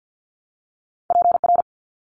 Morse code, heard as a keyed tone, about 50 wpm, 700 Hz. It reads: PR
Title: ARRL Sections 50wpm